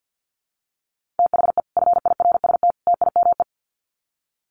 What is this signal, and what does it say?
Morse code, heard as a keyed tone, about 50 wpm, 700 Hz. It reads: THE FIRST TIME
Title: Sets of 3 Words 50wpm